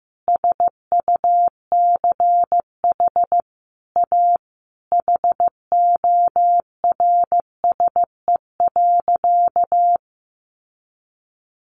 Morse code, heard as a keyed tone, about 15 wpm, 700 Hz. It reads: SUCH A HORSE.
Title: Sentences from Top 1100 Words 15wpm